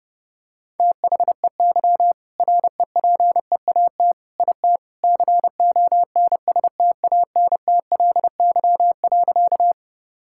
Morse code, heard as a keyed tone, about 32 wpm, 700 Hz. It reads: THEY REPEAT IT CONSTANTLY.
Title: Sentences from Top 1600 Words 30wpm